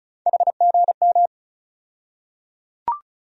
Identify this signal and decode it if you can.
Morse code, heard as a keyed tone, about 36 wpm, 700 Hz. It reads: HGM
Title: USA Call Sign Postfixes - 3 Letters 35wpm